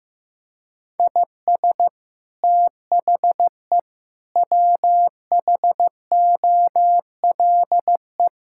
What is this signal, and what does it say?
Morse code, heard as a keyed tone, about 16 wpm, 700 Hz. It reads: IS THE WHOLE
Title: Sentences from Top 1000 Words - Review Part 5 - 15wpm